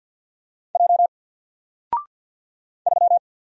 Morse code, heard as a keyed tone, about 50 wpm, 700 Hz. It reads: J 3
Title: Single Character (Letter:Number:Punctuation:Prosign) Alt 50wpm